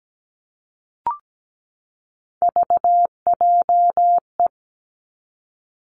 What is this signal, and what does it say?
Morse code, heard as a keyed tone, about 17 wpm, 700 Hz. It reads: VJE
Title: Three Challenging Letter-Number Combinations 17wpm